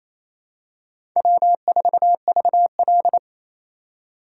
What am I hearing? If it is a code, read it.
Morse code, heard as a keyed tone, about 28 wpm, 700 Hz. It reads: W4VL
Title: USA Call Signs - 4x Character Spacing - 28wpm